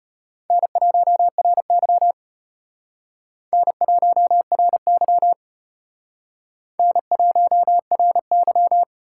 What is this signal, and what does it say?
Morse code, heard as a keyed tone, about 31 wpm, 700 Hz. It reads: N1RY N1RY N1RY
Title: Speed Racing - USA 1x2 Call Signs 25wpm